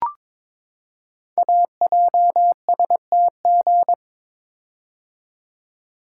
Morse code, heard as a keyed tone, about 22 wpm, 700 Hz. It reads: AJSTG
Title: Code Groups - 5 Letters - 22wpm